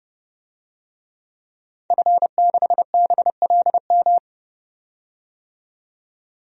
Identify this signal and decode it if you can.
Morse code, heard as a keyed tone, about 30 wpm, 700 Hz. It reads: F6BLM
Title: DXCC-100 Call Signs - 5 Characters - 30wpm